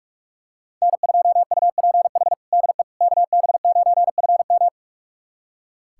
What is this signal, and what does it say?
Morse code, heard as a keyed tone, about 46 wpm, 700 Hz. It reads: N2UPH DE KB9FM
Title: USA Callsign DE Callsign - 5 Characters - 45wpm